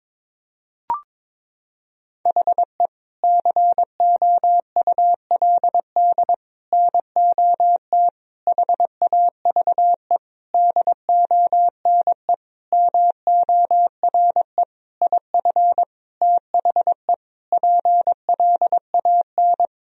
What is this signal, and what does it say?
Morse code, heard as a keyed tone, about 22 wpm, 700 Hz. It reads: HE COULD NOT HAVE DONE MORE IF THE PLAN
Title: Sentences from Top 1000 Words - Review Part 2 - 22wpm